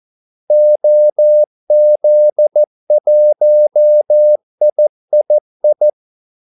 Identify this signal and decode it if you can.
Morse code, heard as a keyed tone, about 14 wpm, 600 Hz. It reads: OZ1III
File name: Callsigns-14wpm